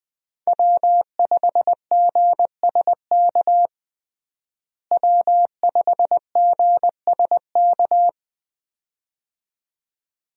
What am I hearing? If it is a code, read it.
Morse code, heard as a keyed tone, about 21 wpm, 700 Hz. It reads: W5GSK W5GSK
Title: USA Call Signs - 5 Characters - Repeated 2x - 20wpm